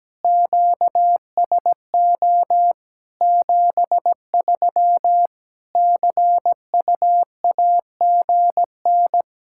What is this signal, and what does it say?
Morse code, heard as a keyed tone, about 17 wpm, 700 Hz. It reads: QSO 73 CUAGN
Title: Practice QSO - Exchange 4 - Ending - 17wpm